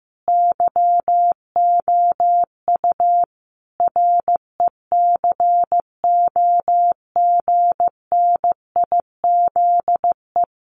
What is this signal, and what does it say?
Morse code, heard as a keyed tone, about 15 wpm, 700 Hz. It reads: YOU RECOGNIZE
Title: Sentences from Top 1400 Words 15wpm